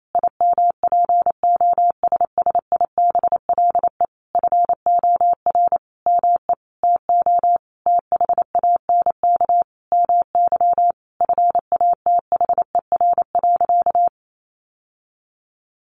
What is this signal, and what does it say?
Morse code, heard as a keyed tone, about 28 wpm, 700 Hz. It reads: IMPOSSIBLE FOR ME TO THANK MY FATHER.
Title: Sentences from Top 1100 Words 28wpm